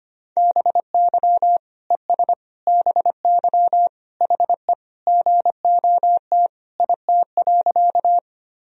Morse code, heard as a keyed tone, about 25 wpm, 700 Hz. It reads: BY ES BY HE GOT IT.
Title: Sentences from Top 300 Words and Abbreviations 25wpm